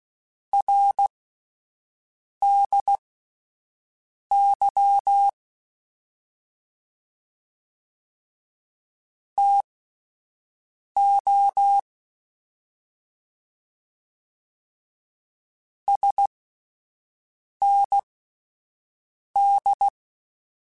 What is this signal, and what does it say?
Morse code, heard as a keyed tone, about 16 wpm, spaced as for 5 wpm, 800 Hz. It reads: RDY TO SND
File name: (spaced) / QSOs Spaced-mp3s / WA7QOX 1 @10 sp